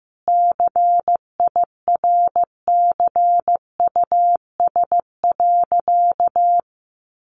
Morse code, heard as a keyed tone, about 14 wpm, 700 Hz. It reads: CIRCUS.
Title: Easy Phrases 15wpm